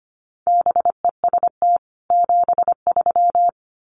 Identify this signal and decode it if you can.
Morse code, heard as a keyed tone, about 25 wpm, 700 Hz. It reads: BEST 73
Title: Common QSO Phrases 25wpm